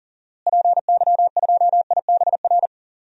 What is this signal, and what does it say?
Morse code, heard as a keyed tone, about 40 wpm, 700 Hz. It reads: PY2IBR
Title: DXCC-100 Call Signs - 6 Characters - Repeated 2x - 40wpm